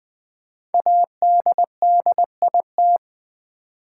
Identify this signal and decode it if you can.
Morse code, heard as a keyed tone, about 20 wpm, 700 Hz. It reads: ADDIT
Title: Build A Word - 5 Letters At A Time - 12-Letter Words - 20wpm